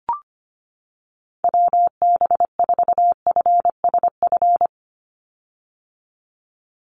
Morse code, heard as a keyed tone, about 25 wpm, 700 Hz. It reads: WB4FSF
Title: Call Signs - 25wpm